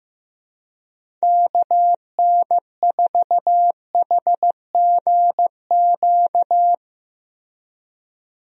Morse code, heard as a keyed tone, about 15 wpm, 700 Hz. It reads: KN4HGQ
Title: USA Call Signs - 6 Characters - Repeated 3x - 15wpm